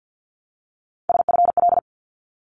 Morse code, heard as a keyed tone, about 50 wpm, 700 Hz. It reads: SFL
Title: ARRL Sections 50wpm